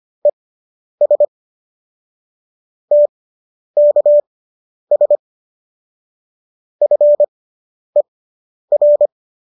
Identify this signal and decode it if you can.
Morse code, heard as a keyed tone, about 25 wpm, spaced as for 10 wpm, 590 Hz. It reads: ES TKS FER